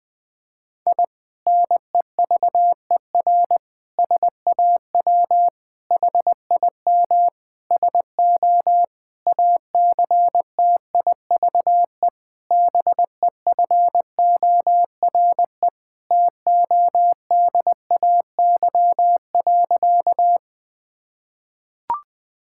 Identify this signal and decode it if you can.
Morse code, heard as a keyed tone, about 21 wpm, 700 Hz. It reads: I NEVER SAW HIM SO ACTIVE BEFORE TODAY.
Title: Sentences from Top 1000 Words - Review Part 6 - 20wpm